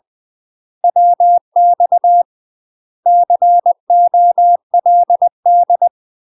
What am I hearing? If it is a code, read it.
Morse code, heard as a keyed tone, about 20 wpm, 700 Hz. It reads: WX COLD
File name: UK MASTER QSOs TRAINING / 20WPM / UK MASTER QSOs 0011